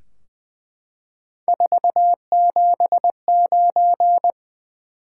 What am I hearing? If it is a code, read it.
Morse code, heard as a keyed tone, about 20 wpm, 700 Hz. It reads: 479
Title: Top 180 QSO Elements 20wpm